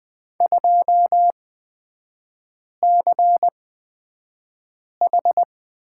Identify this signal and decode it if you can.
Morse code, heard as a keyed tone, about 20 wpm, spaced as for 5 wpm, 700 Hz. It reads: 2CH
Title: 250 Lesson 40e - BK - Three Characters 20wpm